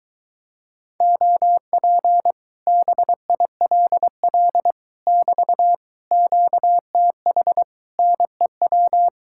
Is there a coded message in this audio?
Yes: Morse code from 1.0 to 9.1 s, heard as a keyed tone, about 23 wpm, 700 Hz. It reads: OP BILL <BT> QTH NEW